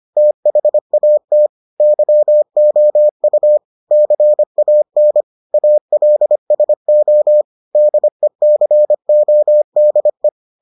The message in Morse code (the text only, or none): THAT YOU CAN ALSO DECODE